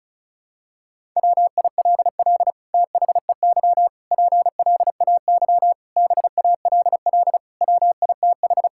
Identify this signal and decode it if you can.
Morse code, heard as a keyed tone, about 36 wpm, 700 Hz. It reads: WILL THEY PLAY BALL WITH
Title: Sentences from Top 1000 Words - Review Part 6 - 35wpm